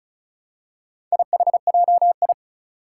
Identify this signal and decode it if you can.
Morse code, heard as a keyed tone, about 36 wpm, 700 Hz. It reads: IHJI
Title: Four Challenging Letter-Number Combinations 35wpm